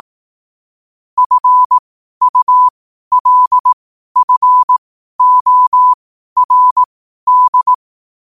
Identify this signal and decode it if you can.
Morse code, heard as a keyed tone, about 18 wpm, spaced as for 12 wpm, 1000 Hz. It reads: FULFORD
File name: qotdfiles / 12wpm